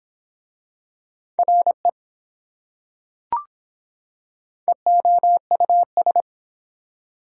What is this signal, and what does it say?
Morse code, heard as a keyed tone, about 24 wpm, 700 Hz. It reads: RE EOUS